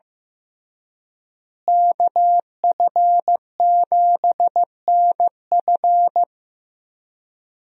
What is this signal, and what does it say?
Morse code, heard as a keyed tone, about 15 wpm, 700 Hz. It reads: KF7NF
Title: USA Call Signs - 5 Characters - Repeated 3x - 15wpm